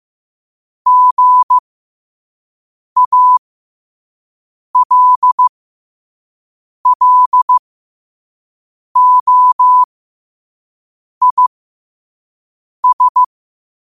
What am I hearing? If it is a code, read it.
Morse code, heard as a keyed tone, about 15 wpm, spaced as for 5 wpm, 1000 Hz. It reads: GALLOIS
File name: qotdfiles / 5wpm